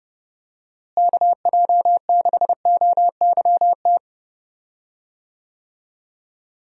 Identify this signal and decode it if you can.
Morse code, heard as a keyed tone, about 30 wpm, 700 Hz. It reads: KJ6OYT
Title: USA Call Signs - 6 Characters - Repeated 2x - 30wpm